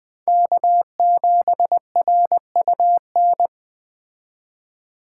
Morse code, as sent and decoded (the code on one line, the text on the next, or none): -.- --... .-. ..- -.
K7RUN